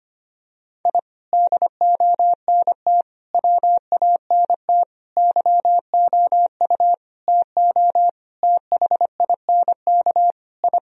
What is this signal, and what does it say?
Morse code, heard as a keyed tone, about 25 wpm, 700 Hz. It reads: I DONT WANT YOU TO THINK I
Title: Sentences from Top 1000 Words - Review Part 3 - 25wpm